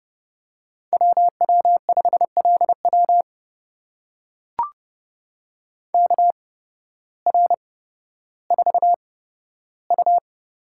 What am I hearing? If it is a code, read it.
Morse code, heard as a keyed tone, about 31 wpm, spaced as for 14 wpm, 700 Hz. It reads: WW5LW KR4U
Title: USA Call Signs - 8x Character Spacing - 30wpm